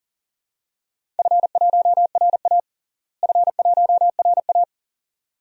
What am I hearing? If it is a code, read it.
Morse code, heard as a keyed tone, about 40 wpm, 700 Hz. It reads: F1RA F1RA
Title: DXCC-100 Call Signs - 4 Characters - Repeated 2x - 40wpm